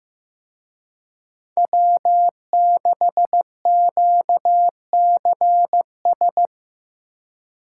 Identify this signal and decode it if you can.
Morse code, heard as a keyed tone, about 18 wpm, 700 Hz. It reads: W6QCS
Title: Speed Racing - USA 1x3 Call Signs 15wpm